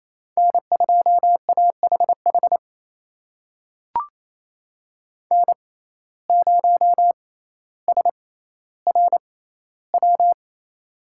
Morse code, heard as a keyed tone, about 28 wpm, spaced as for 14 wpm, 700 Hz. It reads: N2AHH N0SRW
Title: USA Call Signs - 5 Characters - 6x Character Spacing - 28wpm